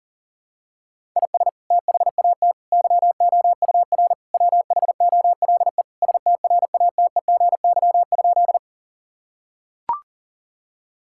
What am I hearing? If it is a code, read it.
Morse code, heard as a keyed tone, about 40 wpm, 700 Hz. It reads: IS THAT YOUR WHOLE STRATEGY?
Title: Sentences from Top 1000 Words - Review Part 5 - 40wpm